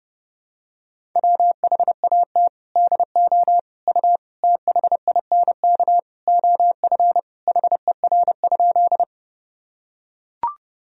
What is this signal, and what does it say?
Morse code, heard as a keyed tone, about 30 wpm, 700 Hz. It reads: WHAT DO U THINK OF HER?
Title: Sentences from Top 200 Words and Abbreviations 30wpm